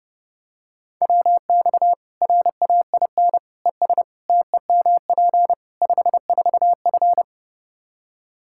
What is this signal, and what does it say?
Morse code, heard as a keyed tone, about 32 wpm, 700 Hz. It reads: WX RAIN ES TEMP 54F
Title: Practice QSO - Exchange 2 - Info - 30wpm